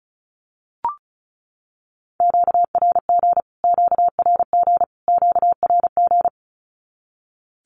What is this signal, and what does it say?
Morse code, heard as a keyed tone, about 34 wpm, 700 Hz. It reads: QRG QRG QRG
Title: Q-Codes Repeated 3x 35wpm